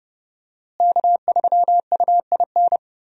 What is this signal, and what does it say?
Morse code, heard as a keyed tone, about 30 wpm, 700 Hz. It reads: K3UIN
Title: USA Call Signs - 5 Characters - Repeated 3x - 30wpm